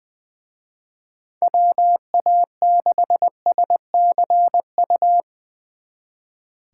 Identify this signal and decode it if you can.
Morse code, heard as a keyed tone, about 22 wpm, 700 Hz. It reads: WA6SCU
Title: Speed Racing - USA 2x3 Call Signs 20wpm